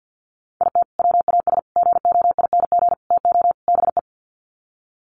Morse code, heard as a keyed tone, about 50 wpm, 700 Hz. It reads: IT WAS GOING TO BE
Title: Sets of 5 Words 50wpm